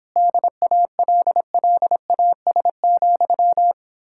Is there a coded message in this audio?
Yes: Morse code from 0.2 to 3.7 s, heard as a keyed tone, about 29 wpm, 700 Hz. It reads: DALLAS,